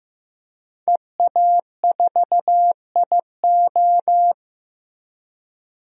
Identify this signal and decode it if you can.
Morse code, heard as a keyed tone, about 15 wpm, 700 Hz. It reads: EA4IO
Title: DXCC-100 Call Signs - 5 Characters - Repeated 2x - 15wpm